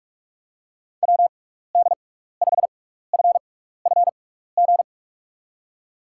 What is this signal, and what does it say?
Morse code, heard as a keyed tone, about 43 wpm, spaced as for 15 wpm, 700 Hz. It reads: WD5FFG
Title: USA Call Signs - 6 Characters - 6x Character Spacing - 45wpm